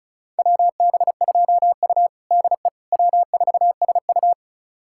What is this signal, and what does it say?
Morse code, heard as a keyed tone, about 36 wpm, 700 Hz. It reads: WB2U DE W4SU
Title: USA Callsign DE Callsign - 4 Characters - 35wpm